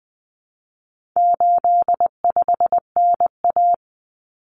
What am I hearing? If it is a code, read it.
Morse code, heard as a keyed tone, about 20 wpm, 700 Hz. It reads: 85NA
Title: Common Cut Numbers - 4-Digit 20wpm